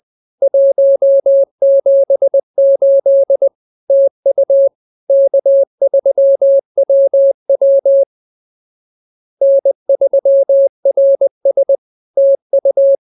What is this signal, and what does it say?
Morse code, heard as a keyed tone, about 20 wpm, 550 Hz. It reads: 178 TU K3WW N3RS TU